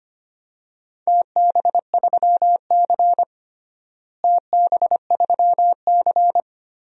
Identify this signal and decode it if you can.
Morse code, heard as a keyed tone, about 25 wpm, 700 Hz. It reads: TB3C TB3C
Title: DXCC-100 Call Signs - 4 Characters - Repeated 2x - 25wpm